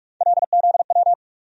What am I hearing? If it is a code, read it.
Morse code, heard as a keyed tone, about 43 wpm, 700 Hz. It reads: LZW
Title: USA Call Sign Postfixes - 3 Letters 45wpm